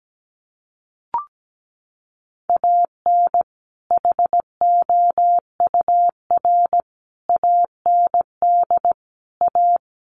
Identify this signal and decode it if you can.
Morse code, heard as a keyed tone, about 17 wpm, 700 Hz. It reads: AN HOUR AND A
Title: Sets of 5 Words 17wpm